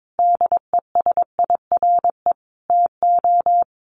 Morse code, heard as a keyed tone, about 22 wpm, 700 Hz. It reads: DESIRE TO
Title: Sentences from Top 1100 Words 22wpm